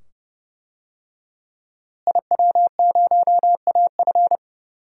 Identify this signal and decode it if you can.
Morse code, heard as a keyed tone, about 30 wpm, 700 Hz. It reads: IW0AF
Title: DXCC-100 Call Signs - 5 Characters - Repeated 2x - 30wpm